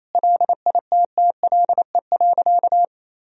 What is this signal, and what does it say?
Morse code, heard as a keyed tone, about 28 wpm, 700 Hz. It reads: LITTLE.
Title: Sentences from Top 1000 Words - Review Part 2 - 28wpm